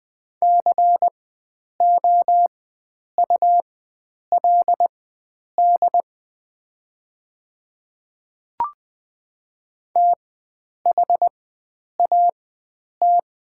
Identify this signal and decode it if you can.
Morse code, heard as a keyed tone, about 21 wpm, spaced as for 9 wpm, 690 Hz. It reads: COULD THAT
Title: 4x Character Spacing - Top 100 Words - 20wpm